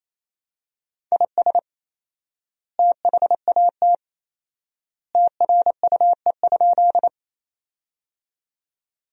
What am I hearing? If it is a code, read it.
Morse code, heard as a keyed tone, about 28 wpm, 700 Hz. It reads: IS THAT TRUE?
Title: 4x Word-Spacing - Sentences from Top 400 Words - 28wpm